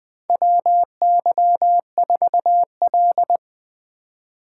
Morse code, heard as a keyed tone, about 20 wpm, 700 Hz. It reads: WY4L
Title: USA Call Signs - 4 Characters - 4x Character Spacing - 20wpm